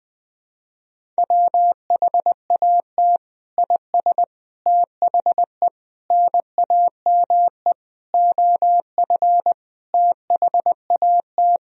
Morse code, heard as a keyed tone, about 20 wpm, 700 Hz. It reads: WHAT IS THE NAME OF THAT
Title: Sentences from Top 1000 Words - Review Part 1 - 20wpm